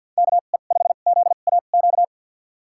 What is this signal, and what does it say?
Morse code, heard as a keyed tone, about 50 wpm, spaced as for 36 wpm, 700 Hz. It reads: KE5ZAQ
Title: USA Call Signs - 2x Character Spacing - 50wpm